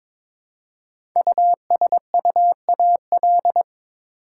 Morse code, heard as a keyed tone, about 22 wpm, 700 Hz. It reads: USUAL
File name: 5 Letter Words - First 3 Letters Given - 22wpm